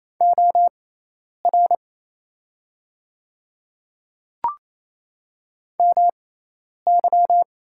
Morse code, heard as a keyed tone, about 28 wpm, spaced as for 10 wpm, 700 Hz. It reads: OR MY
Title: 6x Character Spacing - Top 500 Words - 28wpm